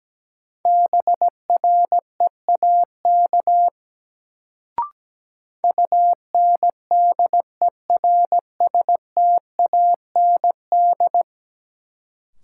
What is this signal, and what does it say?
Morse code, heard as a keyed tone, about 17 wpm, 700 Hz. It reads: BREAK UNDERSTAND
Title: Sets of 4 Words - New Word Drill - 17wpm